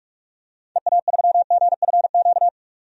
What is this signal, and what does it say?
Morse code, heard as a keyed tone, about 46 wpm, 700 Hz. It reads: EA3GFQ
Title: DXCC-100 Call Signs - 6 Characters - 45wpm